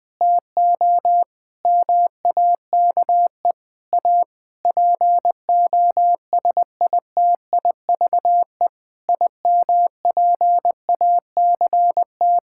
Morse code, heard as a keyed tone, about 21 wpm, 700 Hz. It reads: TO MAKE A POSITIVE IMPACT
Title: Sentences from Top 1000 Words - Review Part 6 - 20wpm